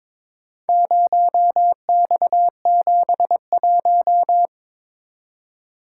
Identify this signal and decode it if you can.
Morse code, heard as a keyed tone, about 23 wpm, 700 Hz. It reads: 0X71
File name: practice / 22wpm / Four Letter-Number Combinations - 22wpm